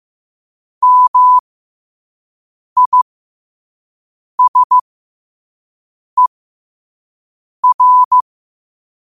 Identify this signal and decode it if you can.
Morse code, heard as a keyed tone, about 15 wpm, spaced as for 5 wpm, 1000 Hz. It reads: MISER
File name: qotdfiles / 5wpm